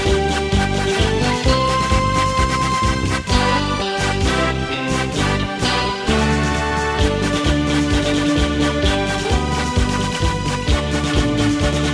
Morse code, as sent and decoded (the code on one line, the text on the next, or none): none